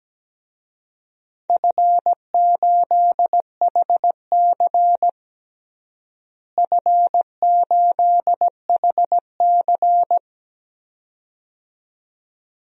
Morse code, heard as a keyed tone, about 17 wpm, 700 Hz. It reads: F8HC F8HC
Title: DXCC-100 Call Signs - 4 Characters - Repeated 2x - 17wpm